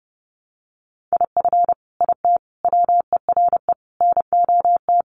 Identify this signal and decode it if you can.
Morse code, heard as a keyed tone, about 30 wpm, 700 Hz. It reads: IF IT WERE NOT
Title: Sets of 5 Words 30wpm